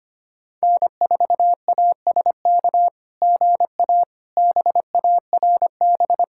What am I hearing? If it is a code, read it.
Morse code, heard as a keyed tone, about 25 wpm, 700 Hz. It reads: N4ASK GA BARB
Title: SOTA QSO Practice 25wpm